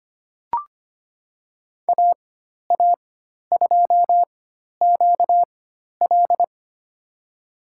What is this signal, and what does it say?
Morse code, heard as a keyed tone, about 25 wpm, spaced as for 12 wpm, 700 Hz. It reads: AA2QL
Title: USA Call Signs - 5 Characters - 4x Character Spacing - 25wpm